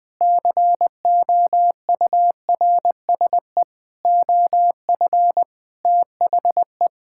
Morse code, heard as a keyed tone, about 21 wpm, 700 Hz. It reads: COURSE OF THE
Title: Sentences with 5-Word Phrases - Spoken Beginning - 20wpm